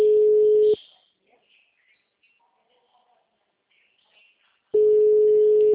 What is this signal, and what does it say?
Ringback tone (Europe, 425 Hz).
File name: out- -2012- - - .54803